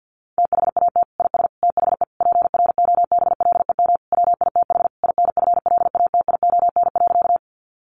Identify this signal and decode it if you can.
Morse code, heard as a keyed tone, about 50 wpm, 700 Hz. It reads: THAT IS THE PROBLEM WITH INFLATION.
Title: Sentences from Top 1100 Words 50wpm